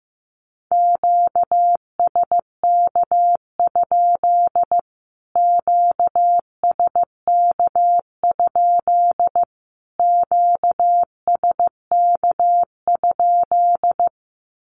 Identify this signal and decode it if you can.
Morse code, heard as a keyed tone, about 15 wpm, 700 Hz. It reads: QSK? QSK? QSK?
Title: Q-Codes Repeated 3x 15wpm